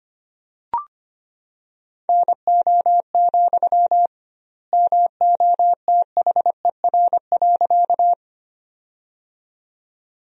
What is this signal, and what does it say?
Morse code, heard as a keyed tone, about 25 wpm, 700 Hz. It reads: NO, MOTHER.
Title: 2x Word-Spacing - Sentences from Top 200 Words - 25wpm